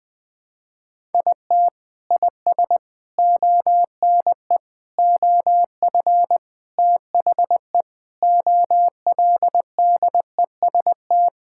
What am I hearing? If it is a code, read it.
Morse code, heard as a keyed tone, about 21 wpm, 700 Hz. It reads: IT IS ONE OF THE OLDEST
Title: Sentences with 5-Word Phrases - Spoken Beginning - 20wpm